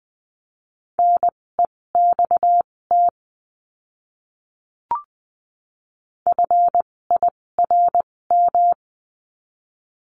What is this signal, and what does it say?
Morse code, heard as a keyed tone, about 20 wpm, spaced as for 16 wpm, 700 Hz. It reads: NEXT FIRM
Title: 256 Lesson 42a - Words 12-20wpm